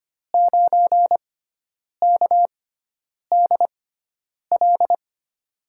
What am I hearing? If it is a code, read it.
Morse code, heard as a keyed tone, about 25 wpm, spaced as for 9 wpm, 700 Hz. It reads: 9KDL
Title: USA Call Signs - 6x Character Spacing - 25wpm